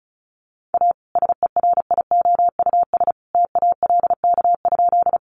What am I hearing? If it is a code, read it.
Morse code, heard as a keyed tone, about 34 wpm, 700 Hz. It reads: A SERIOUS TALK?
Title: Sentences from Top 1100 Words 35wpm